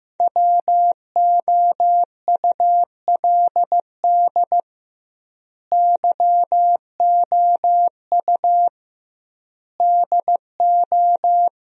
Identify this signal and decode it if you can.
Morse code, heard as a keyed tone, about 15 wpm, 700 Hz. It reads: WOULD YOU DO
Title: 2x Word-Spacing - Sentences from Top 100 Words - 15wpm